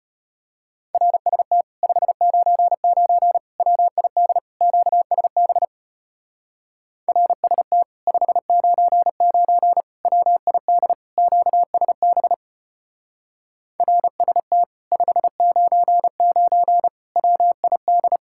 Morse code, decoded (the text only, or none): RST 599 WID QSB RST 599 WID QSB RST 599 WID